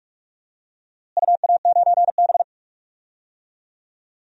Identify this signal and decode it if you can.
Morse code, heard as a keyed tone, about 43 wpm, 700 Hz. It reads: UA9B
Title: DXCC-100 Call Signs - 4 Characters - Repeated 2x - 45wpm